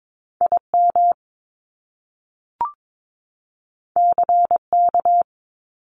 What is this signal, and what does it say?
Morse code, heard as a keyed tone, about 22 wpm, 700 Hz. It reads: IM CK
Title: Top Two Letter Combinations in Words 22wpm